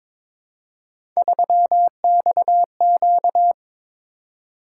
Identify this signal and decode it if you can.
Morse code, heard as a keyed tone, about 22 wpm, 700 Hz. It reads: 3XQ
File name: practice / 22wpm / Three Letter-Number Combinations - 22wpm